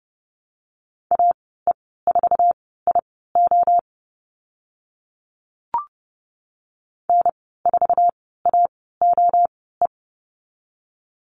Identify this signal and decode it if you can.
Morse code, heard as a keyed tone, about 30 wpm, spaced as for 17 wpm, 700 Hz. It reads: AE4IO N4AOE
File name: Lesson 14g - Number 2 - Call Signs 30wpm